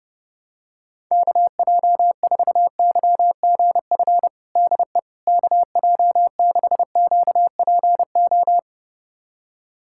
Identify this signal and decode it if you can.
Morse code, heard as a keyed tone, about 32 wpm, 690 Hz. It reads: KJ4YGF DE KJ6QPO
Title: USA Callsign DE Callsign 30wpm